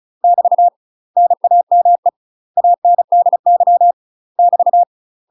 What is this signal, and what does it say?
Morse code, heard as a keyed tone, about 34 wpm, 700 Hz. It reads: <BT> NAME ANDY <BT>
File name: UK MASTER QSOs TRAINING / 35WPM / UK MASTER QSOs 0002